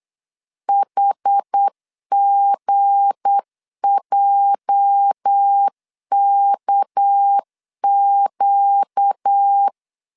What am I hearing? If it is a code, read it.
Morse code, heard as a keyed tone, about 8 wpm, 800 Hz. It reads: HGJKQ
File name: LEZIONI / 8 YJ45 / 8 YJ 8wpm 3313